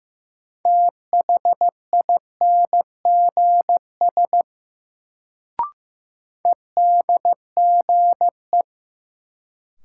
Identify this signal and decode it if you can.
Morse code, heard as a keyed tone, about 15 wpm, 700 Hz. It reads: THINGS EDGE
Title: Sets of 3 Words - New Word Drill - 15wpm